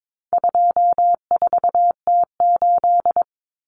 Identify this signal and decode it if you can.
Morse code, heard as a keyed tone, about 22 wpm, 700 Hz. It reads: 24T8
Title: Common Cut Numbers - 4-Digit 22wpm